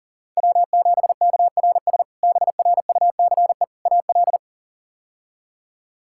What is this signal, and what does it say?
Morse code, heard as a keyed tone, about 40 wpm, 700 Hz. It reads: W7KRS BRUCE AL
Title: K1USN SST QSO Practice - 40wpm